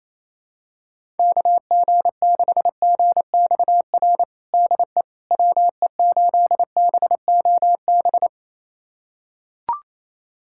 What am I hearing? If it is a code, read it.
Morse code, heard as a keyed tone, about 28 wpm, 700 Hz. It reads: KG6GXR DE WE8BOB
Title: USA Callsign DE Callsign 28wpm